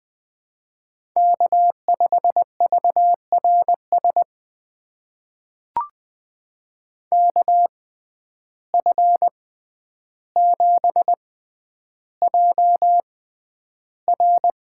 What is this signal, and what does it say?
Morse code, heard as a keyed tone, about 20 wpm, spaced as for 10 wpm, 700 Hz. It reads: K5VRS KF7JR
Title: USA Call Signs - 5 Characters - 6x Character Spacing - 20wpm